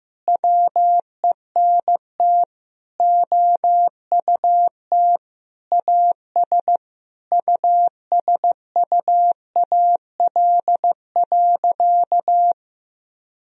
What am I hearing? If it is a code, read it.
Morse code, heard as a keyed tone, about 16 wpm, 700 Hz. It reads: WENT OUT AS USUAL.
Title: Sentences from Top 1000 Words - Review Part 2 - 15wpm